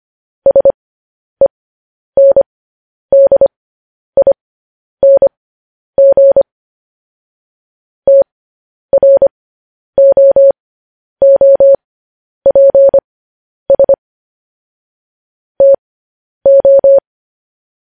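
Morse code, heard as a keyed tone, about 25 wpm, spaced as for 10 wpm, 550 Hz. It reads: SENDING TROOPS TO
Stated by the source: Morse news headlines at 10.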